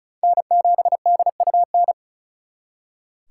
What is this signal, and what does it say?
Morse code, heard as a keyed tone, about 45 wpm, 700 Hz. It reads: N7DUN